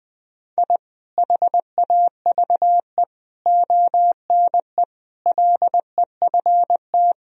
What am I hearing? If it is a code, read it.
Morse code, heard as a keyed tone, about 20 wpm, 700 Hz. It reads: I HAVE ONE LEFT
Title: Sentences from Top 1000 Words - Review Part 4 - 20wpm